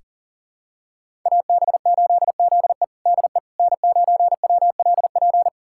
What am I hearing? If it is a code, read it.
Morse code, heard as a keyed tone, about 40 wpm, 700 Hz. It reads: AB8ZE DE N9WLP
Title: USA Callsign DE Callsign - 5 Characters - 40wpm